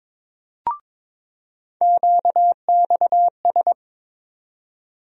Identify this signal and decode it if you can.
Morse code, heard as a keyed tone, about 22 wpm, 700 Hz. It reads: QXS
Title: USA Call Sign Postfixes - 3 Letters 22wpm